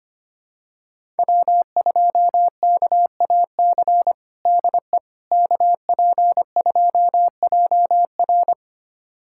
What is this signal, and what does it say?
Morse code, heard as a keyed tone, about 25 wpm, 700 Hz. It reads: W2KAC DE KP2JR
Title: USA Callsign DE Callsign - 5 Characters - 25wpm